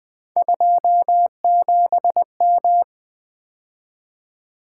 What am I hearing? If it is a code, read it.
Morse code, heard as a keyed tone, about 20 wpm, 700 Hz. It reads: 27M
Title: Three Letter-Number Combinations - 20wpm